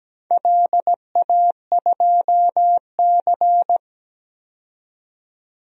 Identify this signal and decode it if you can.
Morse code, heard as a keyed tone, about 17 wpm, 700 Hz. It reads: LA2C
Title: DXCC-100 Call Signs - 4 Characters - 17wpm